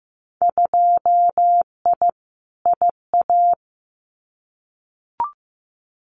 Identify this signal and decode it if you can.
Morse code, heard as a keyed tone, about 15 wpm, 700 Hz. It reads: 2I IA
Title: Winter Field Day Exchanges Encore 15wpm